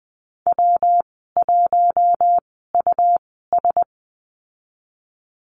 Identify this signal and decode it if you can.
Morse code, heard as a keyed tone, about 20 wpm, spaced as for 15 wpm, 700 Hz. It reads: W1US
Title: 140 Lesson 23g - Letter f - Call Signs 20wpm